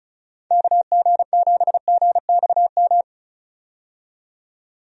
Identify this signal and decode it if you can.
Morse code, heard as a keyed tone, about 34 wpm, 700 Hz. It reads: KG7GXM
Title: USA Call Signs - 6 Characters - 35wpm